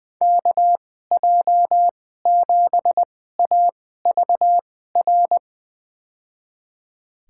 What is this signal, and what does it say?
Morse code, heard as a keyed tone, about 20 wpm, spaced as for 15 wpm, 700 Hz. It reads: KJ7AVR
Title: USA Call Signs - 6 Characters - 2x Character Spacing - 20wpm